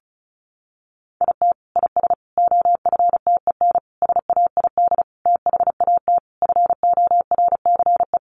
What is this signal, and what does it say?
Morse code, heard as a keyed tone, about 34 wpm, 700 Hz. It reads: IT IS OFTEN SAID THAT FORCE
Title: Sentences from Top 1400 Words 35wpm